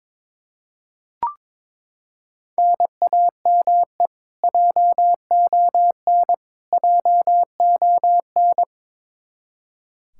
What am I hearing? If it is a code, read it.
Morse code, heard as a keyed tone, about 22 wpm, 700 Hz. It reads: NAME JON JON
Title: K1USN SST QSO Practice 22wpm